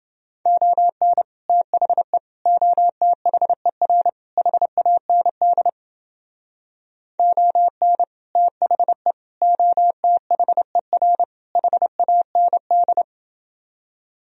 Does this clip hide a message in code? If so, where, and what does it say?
Yes: Morse code from 0.5 to 13.0 s, heard as a keyed tone, about 30 wpm, 700 Hz. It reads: ON THE OTHER HAND ON THE OTHER HAND